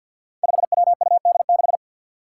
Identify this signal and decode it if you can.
Morse code, heard as a keyed tone, about 52 wpm, 700 Hz. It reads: 5WUD6
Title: Five Challenging Letter-Number Combinations 50wpm